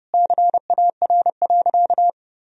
Morse code, heard as a keyed tone, about 32 wpm, 700 Hz. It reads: CAR.
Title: Sentences from Top 1000 Words - Review Part 2 - 30wpm